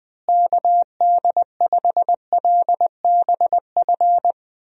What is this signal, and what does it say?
Morse code, heard as a keyed tone, about 20 wpm, 700 Hz. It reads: KD5LBF
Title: USA Call Signs - 6 Characters - Repeated 3x - 20wpm